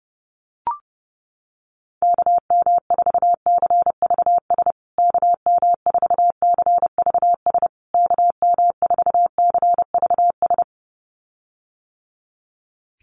Morse code, heard as a keyed tone, about 30 wpm, 700 Hz. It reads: KM4CVS KM4CVS KM4CVS
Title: USA Call Signs Repeated 3x 30wpm